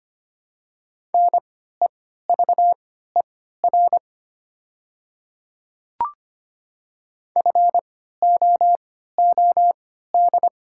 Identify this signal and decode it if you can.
Morse code, heard as a keyed tone, about 25 wpm, spaced as for 14 wpm, 700 Hz. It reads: NEVER FOOD
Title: 3x Character Spacing - Top 500 Words - 25wpm